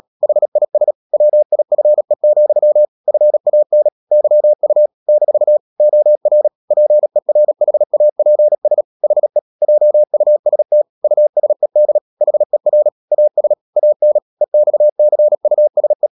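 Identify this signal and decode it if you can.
Morse code, heard as a keyed tone, about 37 wpm, 600 Hz. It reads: HIS WIFE, FAN YU - OR PERHAPS HE JUST USED HER AS AN EXCUSE
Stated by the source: Morse code practice at 35–39 wpm sending English text of Ken Follett's novel Never, from CW med Gnister.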